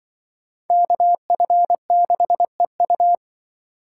Morse code, heard as a keyed tone, about 25 wpm, 700 Hz. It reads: KF6EU
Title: Speed Racing - USA 2x2 Call Signs 20wpm